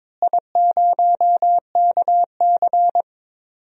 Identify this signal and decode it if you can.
Morse code, heard as a keyed tone, about 22 wpm, 700 Hz. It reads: I0KC
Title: DXCC-100 Call Signs - 4 Characters - Repeated 2x - 22wpm